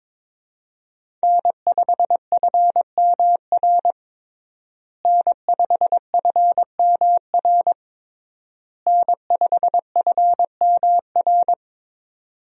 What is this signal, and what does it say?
Morse code, heard as a keyed tone, about 23 wpm, 700 Hz. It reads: N5FMR N5FMR N5FMR
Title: USA Call Signs - 5 Characters - Repeated 3x - 22wpm